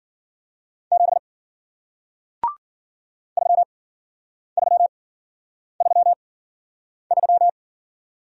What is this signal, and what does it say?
Morse code, heard as a keyed tone, about 53 wpm, spaced as for 9 wpm, 700 Hz. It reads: 7 3333